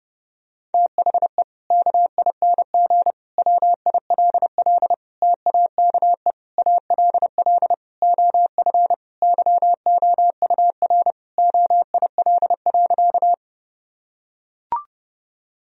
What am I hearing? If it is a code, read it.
Morse code, heard as a keyed tone, about 32 wpm, 700 Hz. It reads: THE KING WILL TAKE ALL OF YOUR OIL.
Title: Sentences from Top 1000 Words - Review Part 4 - 30wpm